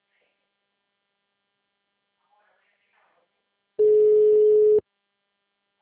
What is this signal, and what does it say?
Ringback tone (Europe, 425 Hz).
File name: out- -2001- - - .62525